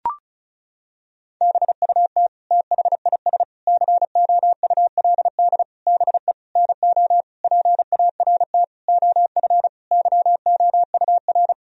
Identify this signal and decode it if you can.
Morse code, heard as a keyed tone, about 36 wpm, 700 Hz. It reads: BUT THIS COULD BE NO PART OF YOUR
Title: Sentences from Top 1000 Words - Review Part 2 - 35wpm